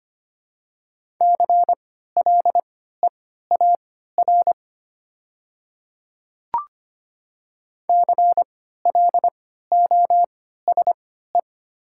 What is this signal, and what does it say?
Morse code, heard as a keyed tone, about 25 wpm, spaced as for 14 wpm, 690 Hz. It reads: CLEAR CLOSE
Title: 3x Character Spacing - Top 500 Words - 25wpm